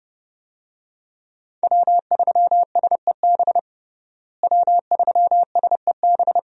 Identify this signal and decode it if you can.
Morse code, heard as a keyed tone, about 30 wpm, 700 Hz. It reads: W3SEB W3SEB
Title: USA Call Signs - 5 Characters - Repeated 2x - 30wpm